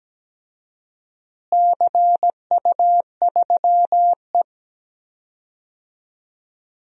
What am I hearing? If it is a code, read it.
Morse code, heard as a keyed tone, about 17 wpm, 700 Hz. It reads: CU3E
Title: DXCC-100 Call Signs - 4 Characters - Repeated 3x - 17wpm